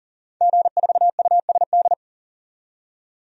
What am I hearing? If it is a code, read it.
Morse code, heard as a keyed tone, about 37 wpm, 700 Hz. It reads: G4USD
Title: DXCC-100 Call Signs - 5 Characters - 40wpm